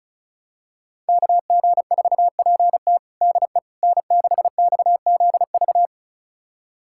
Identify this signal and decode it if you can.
Morse code, heard as a keyed tone, about 36 wpm, 700 Hz. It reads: KG4PT DE N6XZV
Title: USA Callsign DE Callsign - 5 Characters - 35wpm